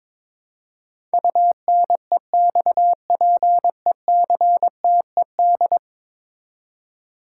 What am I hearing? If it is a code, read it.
Morse code, heard as a keyed tone, about 22 wpm, 700 Hz. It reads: UNEXPECTED
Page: morse-code-ninja-practice-22wpm